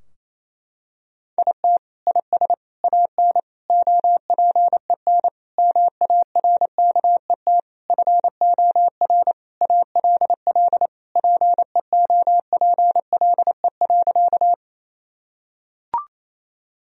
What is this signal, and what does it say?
Morse code, heard as a keyed tone, about 28 wpm, 700 Hz. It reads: IT IS AN OPEN MARKET FOR ALL PEOPLE.
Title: Sentences from Top 1000 Words - Review Part 3 - 28wpm